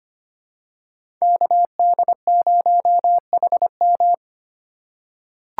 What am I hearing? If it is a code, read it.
Morse code, heard as a keyed tone, about 25 wpm, 700 Hz. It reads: KD0HM
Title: USA Call Signs - 5 Characters - 3x Character Spacing - 25wpm